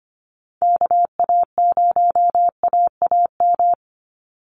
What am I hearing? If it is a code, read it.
Morse code, heard as a keyed tone, about 25 wpm, 700 Hz. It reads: KA0AAM
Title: Call Signs Repeated 2x - 25wpm